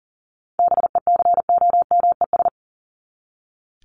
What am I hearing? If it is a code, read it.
Morse code, heard as a keyed tone, about 40 wpm, 700 Hz. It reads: BECOMES
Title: ES:S Suffix - Top 100 Words with Base Words 40wpm